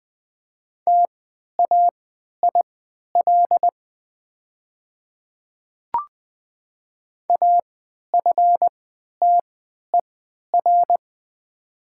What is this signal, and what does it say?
Morse code, heard as a keyed tone, about 20 wpm, spaced as for 11 wpm, 700 Hz. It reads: TAIL AFTER
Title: 3x Character Spacing - Top 500 Words - 20wpm